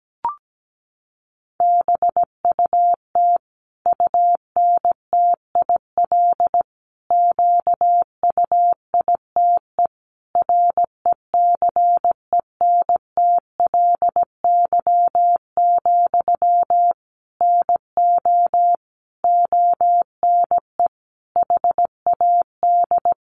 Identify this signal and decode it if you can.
Morse code, heard as a keyed tone, about 17 wpm, 700 Hz. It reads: BUT UNTIL QUITE RECENTLY, NO ONE HAD
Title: Sentences from Top 1200 Words 17wpm